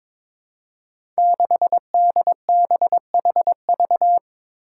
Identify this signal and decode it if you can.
Morse code, heard as a keyed tone, about 22 wpm, 700 Hz. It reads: 6DBHV
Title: Five Challenging Letter-Number Combinations 22wpm